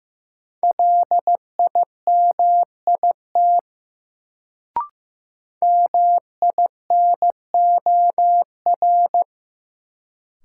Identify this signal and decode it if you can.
Morse code, heard as a keyed tone, about 15 wpm, 700 Hz. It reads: LIMIT MINOR
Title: Sentences from Top 1500 Words - New Word Drill - 15wpm